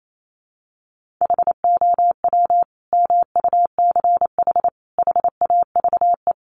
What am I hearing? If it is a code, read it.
Morse code, heard as a keyed tone, about 28 wpm, 700 Hz. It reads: HOW MUCH HAVE
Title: Sentences from Top 300 Words 28wpm